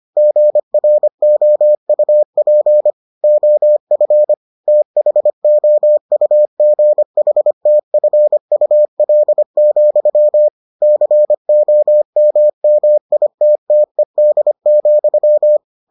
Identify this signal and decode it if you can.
Morse code, heard as a keyed tone, about 25 wpm, 600 Hz. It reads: GROUP OF THOUGHTFUL, COMMITTED,